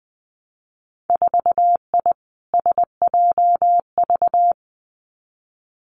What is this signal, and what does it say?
Morse code, heard as a keyed tone, about 19 wpm, 700 Hz. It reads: 4I SJV
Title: Winter Field Day Exchanges Encore 20wpm